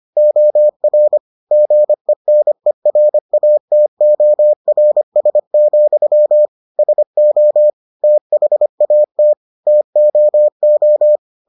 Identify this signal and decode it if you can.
Morse code, heard as a keyed tone, about 25 wpm, 600 Hz. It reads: OR GENERATORS, SO THAT TOO